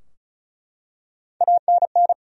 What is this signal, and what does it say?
Morse code, heard as a keyed tone, about 34 wpm, 700 Hz. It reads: ANN
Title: USA Common Names - Max 4 Letters - 35wpm